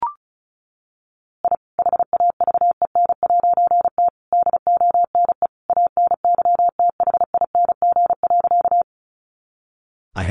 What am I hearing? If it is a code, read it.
Morse code, heard as a keyed tone, about 34 wpm, 700 Hz. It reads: I HAVEN'T DONE ANYTHING.
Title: A Sound Of Thunder 35wpm